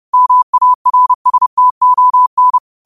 Morse code, heard as a keyed tone, about 30 wpm, 1000 Hz. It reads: MARSTON
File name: qotdfiles / 30wpm